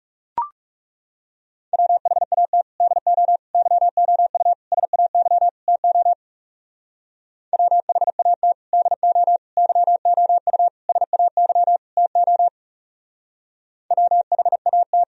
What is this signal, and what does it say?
Morse code, heard as a keyed tone, about 41 wpm, 700 Hz. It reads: WHAT DO YOU SAY TO WHAT DO YOU SAY TO WHAT